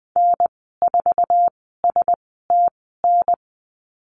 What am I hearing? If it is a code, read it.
Morse code, heard as a keyed tone, about 20 wpm, spaced as for 15 wpm, 700 Hz. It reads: N4STN
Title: 062 Lesson 11g - Letter h - Call Signs 20wpm